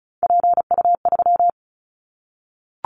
Morse code, heard as a keyed tone, about 36 wpm, 700 Hz. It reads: PU3
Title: Callsign Prefixes for Top 50 DXCC - 35wpm